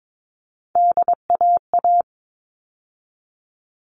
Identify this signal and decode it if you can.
Morse code, heard as a keyed tone, about 22 wpm, 700 Hz. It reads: DAA
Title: All Cut Numbers - 3-Digit 22wpm